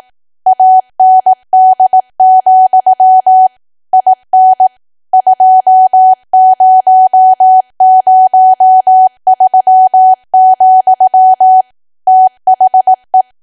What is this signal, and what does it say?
Morse code, heard as a keyed tone, about 18 wpm, 750 Hz. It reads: AND, IN 2003, THE
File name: Archive / 18 WPM / 18WPM